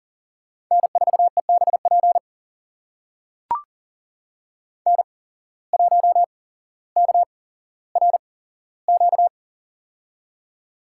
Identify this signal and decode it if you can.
Morse code, heard as a keyed tone, about 38 wpm, spaced as for 17 wpm, 700 Hz. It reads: N4EBP N1KRQ
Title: USA Call Signs - 5 Characters - 8x Character Spacing - 40wpm